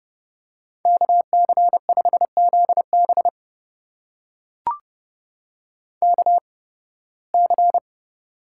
Morse code, heard as a keyed tone, about 30 wpm, spaced as for 20 wpm, 700 Hz. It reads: KC5ZB KC
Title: USA Call Signs - 5 Characters - 8x Character Spacing - 30wpm